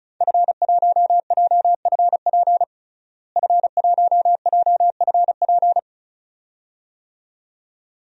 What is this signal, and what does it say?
Morse code, heard as a keyed tone, about 34 wpm, 700 Hz. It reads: F1JFP F1JFP
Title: DXCC-100 Call Signs - 5 Characters - Repeated 2x - 35wpm